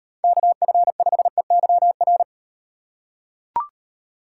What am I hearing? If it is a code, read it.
Morse code, heard as a keyed tone, about 38 wpm, 700 Hz. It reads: KF5EYR